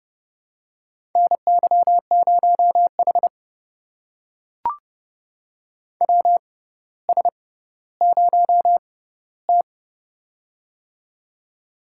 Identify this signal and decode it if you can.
Morse code, heard as a keyed tone, about 30 wpm, spaced as for 15 wpm, 700 Hz. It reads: NY0H WS0T
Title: USA Call Signs - 4 Characters - 6x Character Spacing - 30wpm